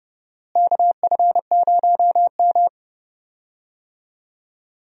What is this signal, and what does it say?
Morse code, heard as a keyed tone, about 31 wpm, 700 Hz. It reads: KF0M
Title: USA Call Signs - 4 Characters - Repeated 2x - 30wpm